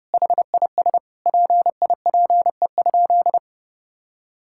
Morse code, heard as a keyed tone, about 30 wpm, 700 Hz. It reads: HIS PIPE?
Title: Sentences from Top 1600 Words 30wpm